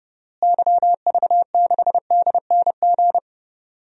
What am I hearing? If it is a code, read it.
Morse code, heard as a keyed tone, about 30 wpm, 700 Hz. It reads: YV6DNG
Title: DXCC-100 Call Signs - 6 Characters - Repeated 2x - 30wpm